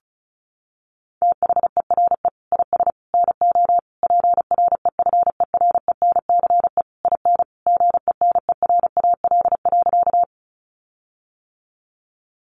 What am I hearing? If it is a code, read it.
Morse code, heard as a keyed tone, about 34 wpm, 700 Hz. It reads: THERE IS NO PREFERENCE IN GENERAL.
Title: Sentences from Top 1500 Words 35wpm